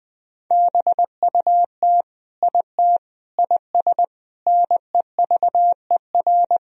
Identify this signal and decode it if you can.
Morse code, heard as a keyed tone, about 21 wpm, 700 Hz. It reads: BUT IT IS NEVER
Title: Sentences from Top 1000 Words - Review Part 2 - 20wpm